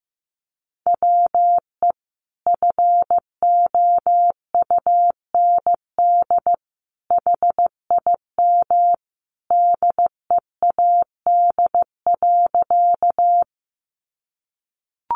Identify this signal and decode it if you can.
Morse code, heard as a keyed tone, about 15 wpm, 700 Hz. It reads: WE FOUND HIM DEAD.
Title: Sentences from Top 1500 Words 15wpm